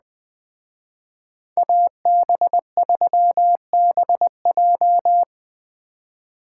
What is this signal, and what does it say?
Morse code, heard as a keyed tone, about 21 wpm, 700 Hz. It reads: AB3BJ
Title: Speed Racing - USA 2x2 Call Signs 17wpm